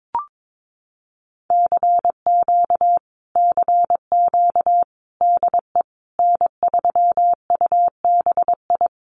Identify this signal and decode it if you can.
Morse code, heard as a keyed tone, about 22 wpm, 700 Hz. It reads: CQ CQ DE N3UBI
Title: Common QSO Phrases Encore 22wpm